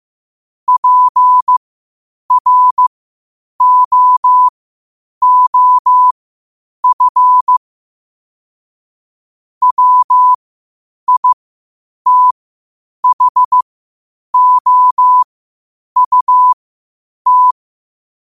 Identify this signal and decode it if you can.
Morse code, heard as a keyed tone, about 15 wpm, spaced as for 8 wpm, 1000 Hz. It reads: PROOF WITHOUT